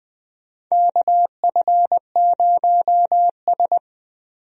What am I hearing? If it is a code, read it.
Morse code, heard as a keyed tone, about 20 wpm, 700 Hz. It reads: KF0S
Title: USA Call Signs - 4 Characters - Repeated 3x - 20wpm